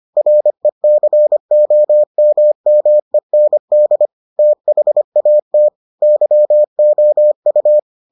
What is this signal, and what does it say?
Morse code, heard as a keyed tone, about 25 wpm, 600 Hz. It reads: RECOMMEND THAT YOU